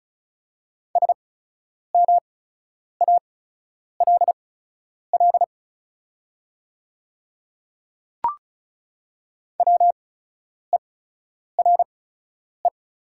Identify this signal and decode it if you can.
Morse code, heard as a keyed tone, about 36 wpm, spaced as for 10 wpm, 700 Hz. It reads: SMALL WERE
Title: 8x Character Spacing - Top 300 Words - 35wpm